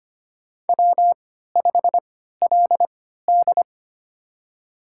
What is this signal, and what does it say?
Morse code, heard as a keyed tone, about 25 wpm, spaced as for 14 wpm, 700 Hz. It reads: W5LD
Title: USA Call Signs - 3x Character Spacing - 25wpm